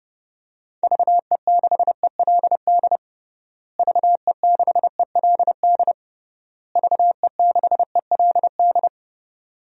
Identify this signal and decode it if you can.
Morse code, heard as a keyed tone, about 30 wpm, 700 Hz. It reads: VE6ELD VE6ELD VE6ELD
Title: DXCC-100 Call Signs - 6 Characters - Repeated 3x - 30wpm